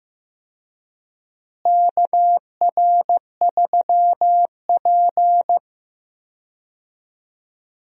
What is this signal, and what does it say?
Morse code, heard as a keyed tone, about 15 wpm, 700 Hz. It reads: KR3P
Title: USA Call Signs - 4 Characters - Repeated 2x - 15wpm